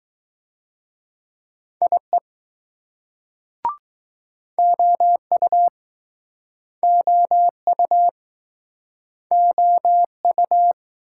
Morse code, heard as a keyed tone, about 20 wpm, 700 Hz. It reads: IE OU OU OU